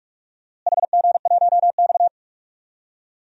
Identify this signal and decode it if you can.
Morse code, heard as a keyed tone, about 41 wpm, 700 Hz. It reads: HG1X
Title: DXCC-100 Call Signs - 4 Characters - Repeated 2x - 45wpm